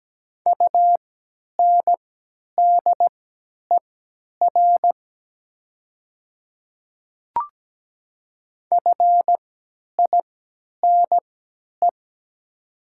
Morse code, heard as a keyed tone, about 17 wpm, spaced as for 10 wpm, 690 Hz. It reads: UNDER FINE
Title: 3x Character Spacing - Top 500 Words - 17wpm